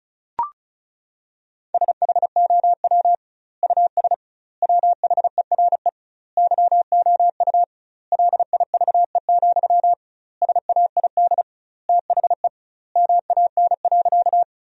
Morse code, heard as a keyed tone, about 34 wpm, 700 Hz. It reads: SHOW US WHERE YOU LIVE, SAID THE MAN.
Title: 2x Word-Spacing - Sentences from Top 200 Words - 35wpm